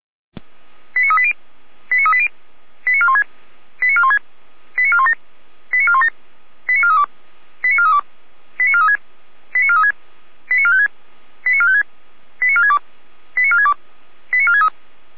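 Five-tone selective call (ZVEI), heard as quick runs of five tones, tones of about 70 ms. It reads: 89380 89380 89416 89416 89417 89417 89432 89432 89436 89436 89456 89456 89462 89462 89463
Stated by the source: ZVEI five-tone selective call received on SDR.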